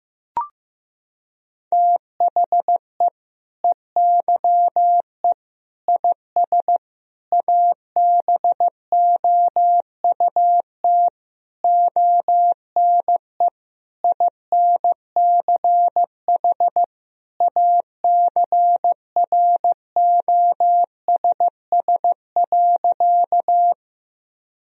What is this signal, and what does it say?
Morse code, heard as a keyed tone, about 15 wpm, 700 Hz. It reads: THE EYE IS ABOUT ONE INCH ACROSS.
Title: Sentences from Top 1000 Words - Review Part 3 - 15wpm